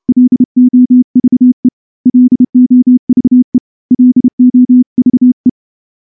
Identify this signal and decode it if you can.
Morse code, heard as a keyed tone, about 29 wpm, 250 Hz. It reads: LOVE LOVE LOVE
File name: morse-training-30WPM